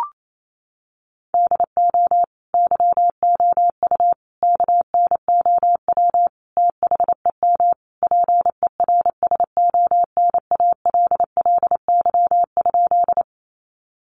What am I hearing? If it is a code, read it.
Morse code, heard as a keyed tone, about 28 wpm, 700 Hz. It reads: DO YOU KNOW THEM PERSONALLY?
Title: Sentences from Top 1400 Words 28wpm